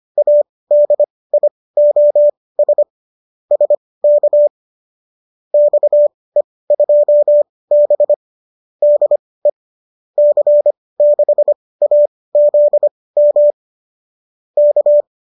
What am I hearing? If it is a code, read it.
Morse code, heard as a keyed tone, about 25 wpm, spaced as for 18 wpm, 590 Hz. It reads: ADIOS SK XE2B DE C6AZM K